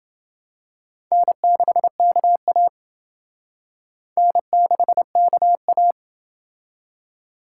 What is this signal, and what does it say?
Morse code, heard as a keyed tone, about 28 wpm, 700 Hz. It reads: N6KA N6KA